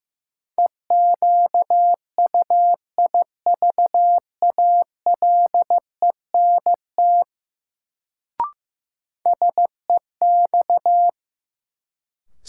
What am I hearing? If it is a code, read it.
Morse code, heard as a keyed tone, about 15 wpm, 700 Hz. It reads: EQUIVALENT SEX
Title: Sentences from Top 1600 Words - New Word Drill Alt - 15wpm